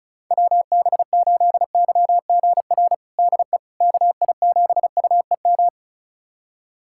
Morse code, heard as a keyed tone, about 34 wpm, 700 Hz. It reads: WB8YGR DE KI7UEM
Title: USA Callsign DE Callsign - 6 Characters - 35wpm